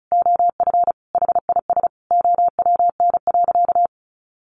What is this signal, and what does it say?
Morse code, heard as a keyed tone, about 34 wpm, 700 Hz. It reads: OF HIS OWN.
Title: Sentences from Top 300 Words 35wpm